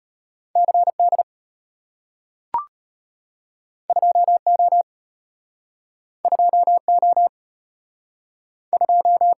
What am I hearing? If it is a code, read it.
Morse code, heard as a keyed tone, about 34 wpm, 700 Hz. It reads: CD 2O 2O 2